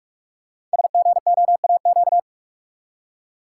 Morse code, heard as a keyed tone, about 44 wpm, 700 Hz. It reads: SGOAQ
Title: Code Groups - 5 Letters - 45wpm